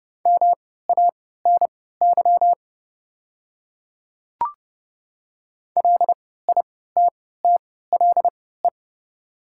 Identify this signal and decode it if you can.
Morse code, heard as a keyed tone, about 30 wpm, spaced as for 17 wpm, 700 Hz. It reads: MANY LITTLE
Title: 3x Character Spacing - Top 200 Words - 30wpm